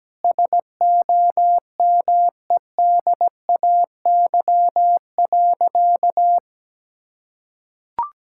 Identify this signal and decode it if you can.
Morse code, heard as a keyed tone, about 17 wpm, 700 Hz. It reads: SOMEDAY.
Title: Sentences from Top 1600 Words 17wpm